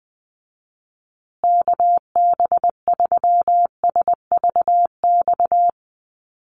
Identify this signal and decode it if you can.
Morse code, heard as a keyed tone, about 20 wpm, 700 Hz. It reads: KB3SVX
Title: USA Call Signs Repeated 2x 20wpm